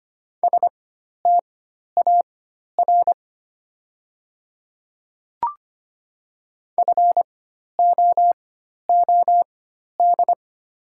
Morse code, heard as a keyed tone, about 25 wpm, spaced as for 12 wpm, 700 Hz. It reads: STAR FOOD
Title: 4x Character Spacing - Top 500 Words - 25wpm